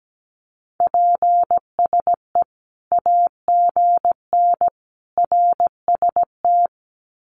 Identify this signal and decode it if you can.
Morse code, heard as a keyed tone, about 17 wpm, 700 Hz. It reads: PSE AGN RST
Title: Common QSO Phrases 17wpm